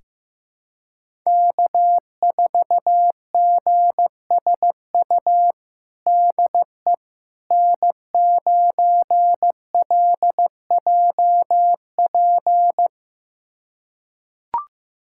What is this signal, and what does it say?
Morse code, heard as a keyed tone, about 16 wpm, 700 Hz. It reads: K4GSU DE N9LJP
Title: USA Callsign DE Callsign 15wpm